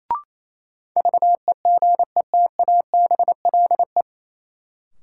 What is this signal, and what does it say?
Morse code, heard as a keyed tone, about 28 wpm, 700 Hz. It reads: VEGETABLE
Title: Sentences from Top 1300 Words - New Word Drill Alt - 28wpm